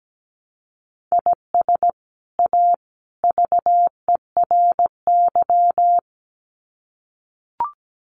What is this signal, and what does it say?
Morse code, heard as a keyed tone, about 17 wpm, 700 Hz. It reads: IS A VERY
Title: Sets of 3 Words 17wpm